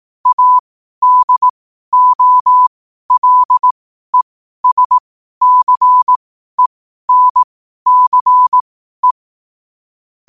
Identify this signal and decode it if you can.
Morse code, heard as a keyed tone, about 18 wpm, spaced as for 12 wpm, 1000 Hz. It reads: ADOLESCENCE
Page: New quotes every day in morse code at 12 Words per minute.